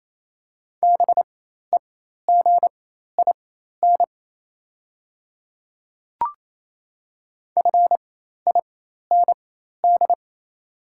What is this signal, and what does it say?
Morse code, heard as a keyed tone, about 27 wpm, spaced as for 13 wpm, 690 Hz. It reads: BEGIN FIND
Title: 4x Character Spacing - Top 300 Words - 28wpm